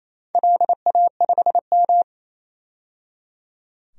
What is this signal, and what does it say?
Morse code, heard as a keyed tone, about 28 wpm, 700 Hz. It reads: LA5M
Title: DXCC-100 Call Signs - 4 Characters - 28wpm